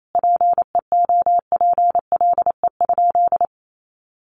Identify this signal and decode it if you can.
Morse code, heard as a keyed tone, about 28 wpm, 700 Hz. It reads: PEOPLE?
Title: Sentences from Top 200 Words 28wpm